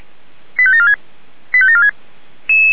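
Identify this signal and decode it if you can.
Five-tone selective call (ZVEI), heard as quick runs of five tones, tones of about 70 ms. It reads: 85746 85746
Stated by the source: ZVEI five-tone selective call received on an SDR, fire brigade alerting.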